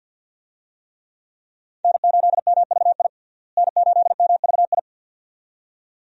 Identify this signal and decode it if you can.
Morse code, heard as a keyed tone, about 52 wpm, 700 Hz. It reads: N8MVI N8MVI
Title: USA Call Signs - 5 Characters - Repeated 2x - 50wpm